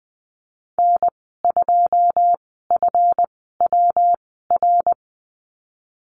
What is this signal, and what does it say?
Morse code, heard as a keyed tone, about 20 wpm, spaced as for 15 wpm, 700 Hz. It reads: N2FWR
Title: 186 Lesson 30g - Letter b - Call Signs 20wpm